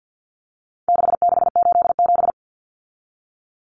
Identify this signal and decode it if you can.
Morse code, heard as a keyed tone, about 50 wpm, 700 Hz. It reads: 6687
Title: Common Cut Numbers - 4-Digit 50wpm